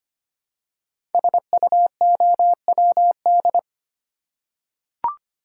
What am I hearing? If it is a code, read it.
Morse code, heard as a keyed tone, about 25 wpm, 700 Hz. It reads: SUOWD
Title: Code Groups - 5 Letters - 25wpm